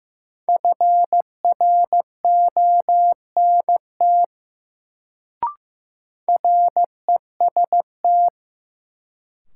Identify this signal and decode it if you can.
Morse code, heard as a keyed tone, about 15 wpm, 700 Hz. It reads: FRONT REST
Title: Sentences from Top 500 Words - New Word Drill - 15wpm